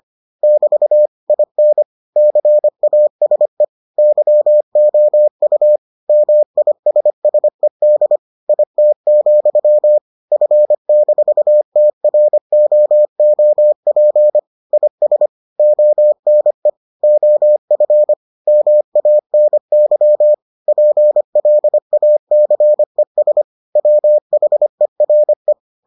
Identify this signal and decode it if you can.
Morse code, heard as a keyed tone, about 25 wpm, 600 Hz. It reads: <BT> IN CASE YOU MISSED IT, F-TROOP IS ONE OF MANY PLACES WHERE